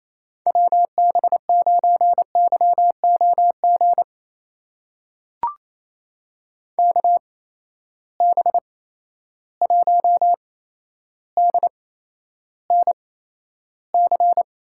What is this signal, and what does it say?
Morse code, heard as a keyed tone, about 28 wpm, spaced as for 12 wpm, 700 Hz. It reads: WB9YOG KB1DNC
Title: USA Call Signs - 6 Characters - 8x Character Spacing - 28wpm